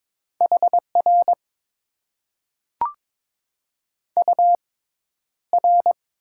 Morse code, heard as a keyed tone, about 22 wpm, spaced as for 11 wpm, 700 Hz. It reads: HR UR
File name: Top 90 QSO Elements - 6x Character Spacing - 22wpm